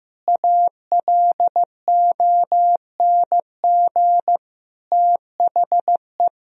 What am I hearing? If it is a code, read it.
Morse code, heard as a keyed tone, about 16 wpm, 700 Hz. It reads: ALONG THE
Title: Sentences from Top 1000 Words - Review Part 4 - 15wpm